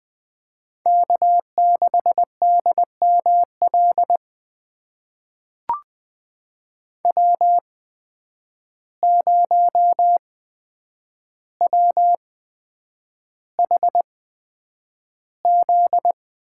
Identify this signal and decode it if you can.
Morse code, heard as a keyed tone, about 20 wpm, spaced as for 9 wpm, 700 Hz. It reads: K6DML W0WHZ
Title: USA Call Signs - 8x Character Spacing - 20wpm